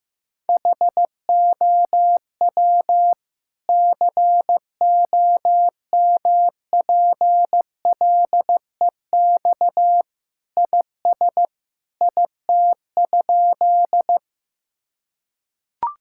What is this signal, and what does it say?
Morse code, heard as a keyed tone, about 16 wpm, 700 Hz. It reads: HOW COMPLEX IS IT?
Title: Sentences from Top 1000 Words - Review Part 6 - 15wpm